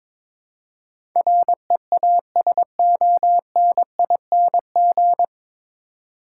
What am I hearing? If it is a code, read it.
Morse code, heard as a keyed tone, about 22 wpm, 700 Hz. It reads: REASONING
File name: 9 Letter Words - First 3 Letters Given - 22wpm